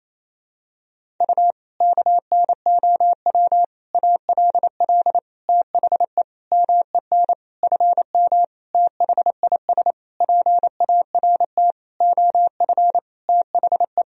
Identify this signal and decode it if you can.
Morse code, heard as a keyed tone, about 28 wpm, 700 Hz. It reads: U KNOW ALL THE MEN FM THIS PART OF THE
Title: Sentences from Top 300 Words and Abbreviations 28wpm